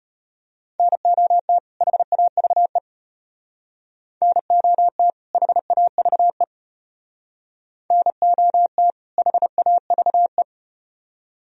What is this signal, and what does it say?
Morse code, heard as a keyed tone, about 35 wpm, 700 Hz. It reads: NOT HAVE NOT HAVE NOT HAVE